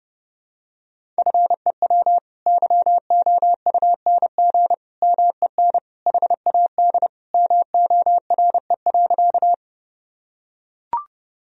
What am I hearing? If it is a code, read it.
Morse code, heard as a keyed tone, about 30 wpm, 700 Hz. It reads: FEW YOUNG MEN HAD MORE.
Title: Sentences from Top 1000 Words - Review Part 1 - 30wpm